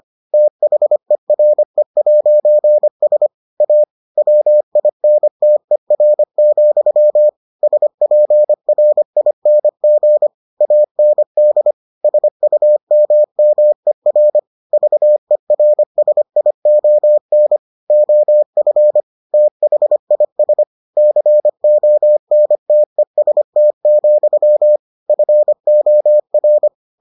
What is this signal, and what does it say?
Morse code, heard as a keyed tone, about 25 wpm, 600 Hz. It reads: THERE'S A WINTER, SPRING AND SUMMER VERSION OF THIS CONTEST, FOR